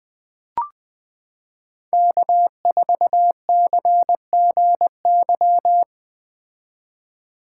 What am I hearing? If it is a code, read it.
Morse code, heard as a keyed tone, about 21 wpm, 700 Hz. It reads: K4CGY
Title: 2023 POTA Hunter Call Signs - 20wpm